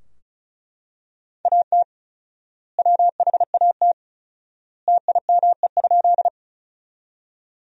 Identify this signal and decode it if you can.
Morse code, heard as a keyed tone, about 34 wpm, 700 Hz. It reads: AT WHAT TIME?
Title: 4x Word-Spacing - Sentences from Top 100 Words - 35wpm